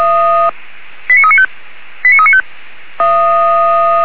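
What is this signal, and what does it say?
Five-tone selective call (ZVEI), heard as quick runs of five tones, tones of about 70 ms. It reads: 89395 89395
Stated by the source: ZVEI five-tone selective call received on SDR.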